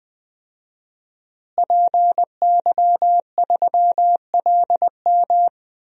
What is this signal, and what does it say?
Morse code, heard as a keyed tone, about 20 wpm, 700 Hz. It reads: PY3LM
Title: DXCC-100 Call Signs - 5 Characters - Repeated 3x - 20wpm